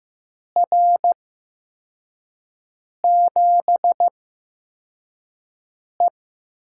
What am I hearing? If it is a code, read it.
Morse code, heard as a keyed tone, about 15 wpm, spaced as for 4 wpm, 700 Hz. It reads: R7E
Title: USA Call Signs - 4 Characters - 8x Character Spacing - 15wpm